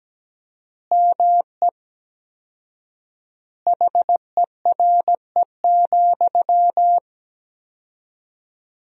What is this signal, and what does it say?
Morse code, heard as a keyed tone, about 17 wpm, 700 Hz. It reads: ME HERE,
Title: 4x Word-Spacing - Sentences from Top 500 Words - 17wpm